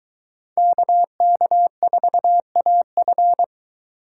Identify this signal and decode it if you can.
Morse code, heard as a keyed tone, about 24 wpm, 700 Hz. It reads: KK4AF